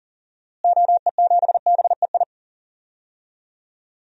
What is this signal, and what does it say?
Morse code, heard as a keyed tone, about 39 wpm, 700 Hz. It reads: OE7BEI
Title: DXCC-100 Call Signs - 6 Characters - Repeated 2x - 40wpm